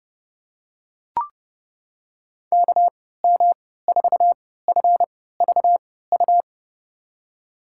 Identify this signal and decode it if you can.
Morse code, heard as a keyed tone, about 30 wpm, spaced as for 17 wpm, 700 Hz. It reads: KM4FVU
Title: USA Call Signs - 6 Characters - 3x Character Spacing - 30wpm